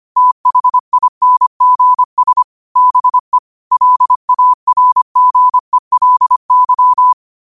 Morse code, heard as a keyed tone, about 25 wpm, 1000 Hz. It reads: THINGS BE LARGELY
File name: qotdfiles / 25wpm